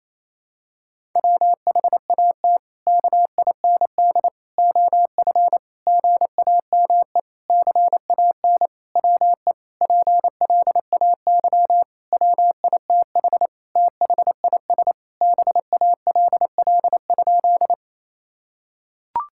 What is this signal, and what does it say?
Morse code, heard as a keyed tone, about 28 wpm, 700 Hz. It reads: WHAT KIND OF GAME CAN WE PLAY WITH THIS BALL?
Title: Sentences from Top 1000 Words - Review Part 6 - 28wpm